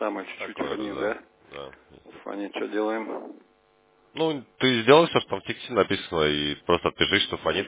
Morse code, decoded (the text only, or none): none